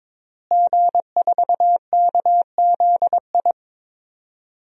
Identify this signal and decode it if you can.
Morse code, heard as a keyed tone, about 22 wpm, 700 Hz. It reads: G4KZI
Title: DXCC-100 Call Signs - 5 Characters - Repeated 2x - 22wpm